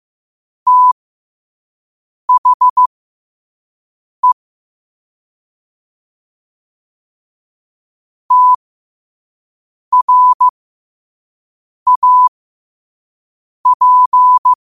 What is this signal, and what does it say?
Morse code, heard as a keyed tone, about 15 wpm, spaced as for 5 wpm, 1000 Hz. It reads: THE TRAP
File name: qotdfiles / 5wpm